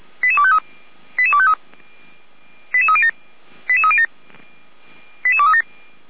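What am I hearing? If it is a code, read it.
Five-tone selective call (ZVEI), heard as quick runs of five tones, tones of about 70 ms. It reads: 88253 88253 88307 88307 88237
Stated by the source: ZVEI five-tone selective call received with an SDR by the fire brigade (Feuerwehr) Orlamünde.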